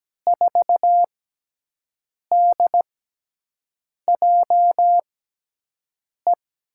Morse code, heard as a keyed tone, about 17 wpm, spaced as for 6 wpm, 700 Hz. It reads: 4DJE
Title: USA Call Signs - 6 Characters - 6x Character Spacing - 17wpm